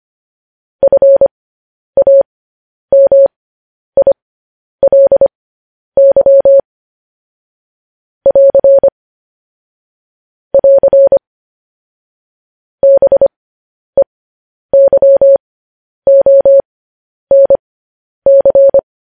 Morse code, heard as a keyed tone, about 25 wpm, spaced as for 10 wpm, 550 Hz. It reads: FAMILY <AR> <AR> BEYONC
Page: News Headlines in Morse Code at 10 WPM